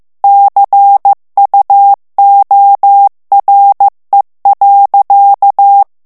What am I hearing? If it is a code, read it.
Morse code, heard as a keyed tone, about 15 wpm, 800 Hz. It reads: CUORE.
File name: ITALIEN texte clair 1 vitesse 14 mots minute